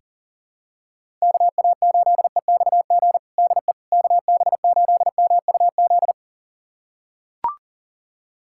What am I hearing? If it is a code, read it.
Morse code, heard as a keyed tone, about 40 wpm, 700 Hz. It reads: KA8EXG DE KB8MUZ
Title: USA Callsign DE Callsign 40wpm